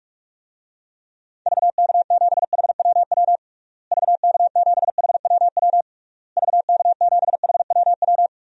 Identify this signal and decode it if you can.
Morse code, heard as a keyed tone, about 46 wpm, 700 Hz. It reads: VK7HWW VK7HWW VK7HWW
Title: DXCC-100 Call Signs - 6 Characters - Repeated 3x - 45wpm